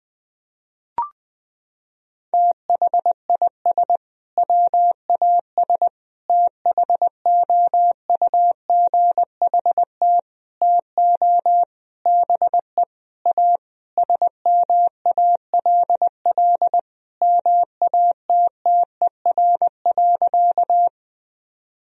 Morse code, heard as a keyed tone, about 21 wpm, 700 Hz. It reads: THIS WAS THOUGHT TO BE A SMALL MATTER.
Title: Sentences from Top 1000 Words - Review Part 1 - 20wpm